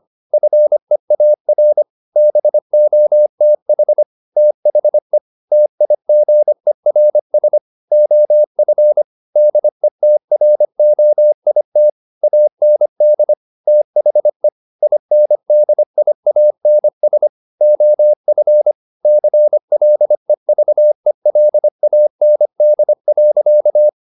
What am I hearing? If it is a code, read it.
Morse code, heard as a keyed tone, about 25 wpm, 600 Hz. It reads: FEAR BOTH THE TIGERS OF DETROIT AND THE INDIANS OF CLEVELAND.
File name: the old man and the sea part 1 25wpm